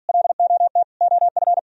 Morse code, heard as a keyed tone, about 53 wpm, 700 Hz. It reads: LOT OF